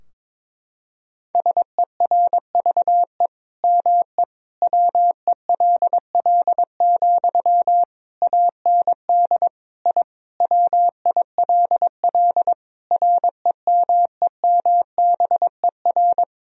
Morse code, heard as a keyed tone, about 22 wpm, 700 Hz. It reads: SERVE ME WELL, AND I WILL REMEMBER
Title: Sentences from Top 1000 Words - Review Part 2 - 22wpm